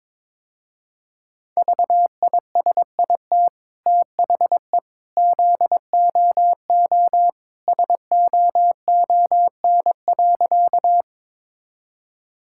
Morse code, heard as a keyed tone, about 22 wpm, 700 Hz. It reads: VISIT THE ZOO SOON.
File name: practice / 22wpm / Easy Sentences with Hints - 22wpm